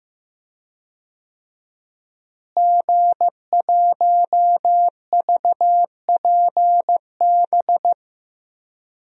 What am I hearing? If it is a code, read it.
Morse code, heard as a keyed tone, about 15 wpm, 700 Hz. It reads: G1VPB
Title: DXCC-100 Call Signs - 5 Characters - Repeated 3x - 15wpm